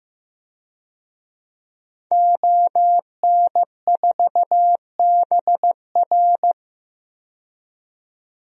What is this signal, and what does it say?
Morse code, heard as a keyed tone, about 15 wpm, 700 Hz. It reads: ON4BR
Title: DXCC-100 Call Signs - 5 Characters - 15wpm